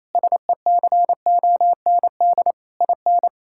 Morse code, heard as a keyed tone, about 28 wpm, 700 Hz. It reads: SECOND IN
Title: Sentences from Top 1600 Words 28wpm